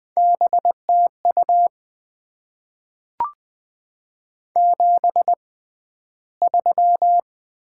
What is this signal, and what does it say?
Morse code, heard as a keyed tone, about 20 wpm, spaced as for 12 wpm, 690 Hz. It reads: BTU 73
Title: Top 20 QSO Elements - 6x Character Spacing - 20wpm